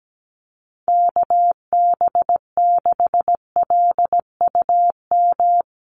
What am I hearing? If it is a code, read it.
Morse code, heard as a keyed tone, about 17 wpm, 700 Hz. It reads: KB6LUM
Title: USA Call Signs Repeated 2x 17wpm